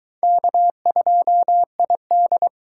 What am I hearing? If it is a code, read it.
Morse code, heard as a keyed tone, about 22 wpm, 700 Hz. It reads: K2ID